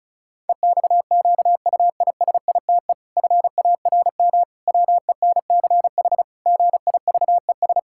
Morse code, heard as a keyed tone, about 36 wpm, 700 Hz. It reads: EXQUISITE FARM WENCH GIVES
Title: Pangram Challenge v2 - 35wpm